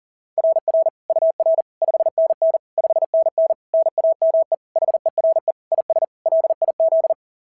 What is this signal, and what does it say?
Morse code, heard as a keyed tone, about 40 wpm, 690 Hz. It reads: RR UR 5NN 5NN NAME HERE IS LIZ
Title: SOTA QSO Practice 40wpm